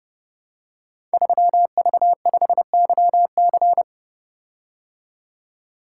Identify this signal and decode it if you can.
Morse code, heard as a keyed tone, about 28 wpm, 700 Hz. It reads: 3V5YC
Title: Code Groups - 5 Letter:Numbers - 30wpm